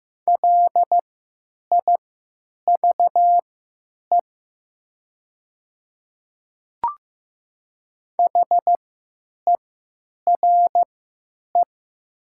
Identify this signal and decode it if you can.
Morse code, heard as a keyed tone, about 15 wpm, spaced as for 9 wpm, 700 Hz. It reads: LIVE HERE
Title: 3x Character Spacing - Top 200 Words - 15wpm